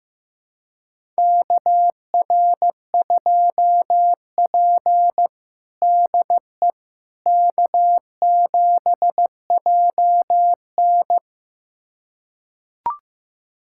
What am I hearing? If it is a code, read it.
Morse code, heard as a keyed tone, about 15 wpm, 700 Hz. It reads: KR2P DE K7JN
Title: USA Callsign DE Callsign - 4 Characters - 15wpm